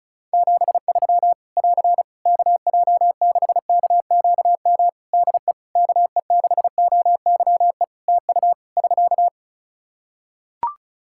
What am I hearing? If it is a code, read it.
Morse code, heard as a keyed tone, about 34 wpm, 700 Hz. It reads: 73 <AR> KJ6KQM DE KE6OYE TU <SK>
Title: Practice QSO - Exchange 4 - Ending - 35wpm